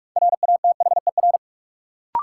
Morse code, heard as a keyed tone, about 46 wpm, 700 Hz. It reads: RATHER
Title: Sentences from Top 700 Words - New Word Drill - 45wpm